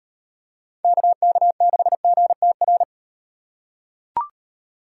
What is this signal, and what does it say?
Morse code, heard as a keyed tone, about 38 wpm, 700 Hz. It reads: KK6GTR